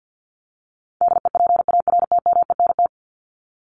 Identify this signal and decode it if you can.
Morse code, heard as a keyed tone, about 50 wpm, 700 Hz. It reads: DEPARTMENT
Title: Top 751-1000 Words 50wpm